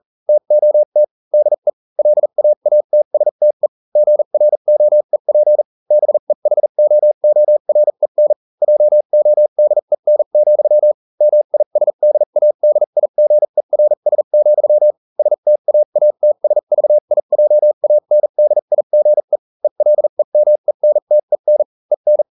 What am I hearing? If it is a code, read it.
Morse code, heard as a keyed tone, about 40 wpm, spaced as for 35 wpm, 600 Hz. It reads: TOT DE LAATSTE GROEP BEHOOREN JODEN, MISDADIGERS, STAATSVIJANDIGE ELEMENTEN EN